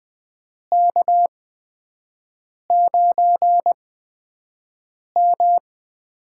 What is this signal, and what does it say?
Morse code, heard as a keyed tone, about 20 wpm, spaced as for 5 wpm, 700 Hz. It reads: K9M
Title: USA Call Signs - 4 Characters - 8x Character Spacing - 20wpm